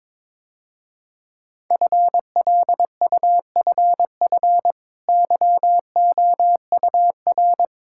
Morse code, heard as a keyed tone, about 21 wpm, 690 Hz. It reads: FLUFF YOUR
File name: practice / 22wpm / Easy Sentences with Hints - 22wpm